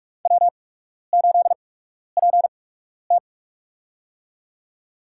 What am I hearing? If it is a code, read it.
Morse code, heard as a keyed tone, about 47 wpm, spaced as for 12 wpm, 690 Hz. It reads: W8PT
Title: USA Call Signs - 4 Characters - 8x Character Spacing - 45wpm